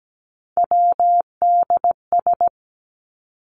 Morse code, heard as a keyed tone, about 17 wpm, 700 Hz. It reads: WDS
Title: Abbreviations 17wpm